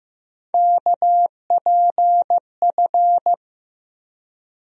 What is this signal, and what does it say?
Morse code, heard as a keyed tone, about 15 wpm, 700 Hz. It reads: KPF
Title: USA Call Sign Postfixes - 3 Letters 15wpm